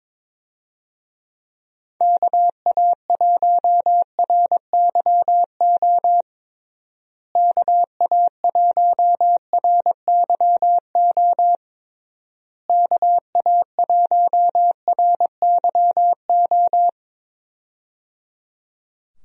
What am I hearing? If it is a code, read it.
Morse code, heard as a keyed tone, about 21 wpm, 700 Hz. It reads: KA1RYO KA1RYO KA1RYO
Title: USA Call Signs - 6 Characters - Repeated 3x - 22wpm